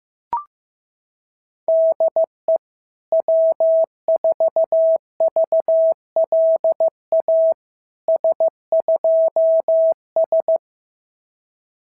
Morse code, heard as a keyed tone, about 14 wpm, 650 Hz. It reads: DE W4VLA S2S
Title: SOTA QSO Practice 15wpm